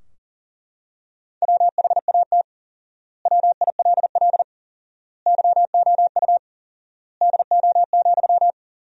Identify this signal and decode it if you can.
Morse code, heard as a keyed tone, about 40 wpm, 700 Hz. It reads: WHAT WILL YOU DO,
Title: 4x Word-Spacing - Sentences from Top 100 Words - 40wpm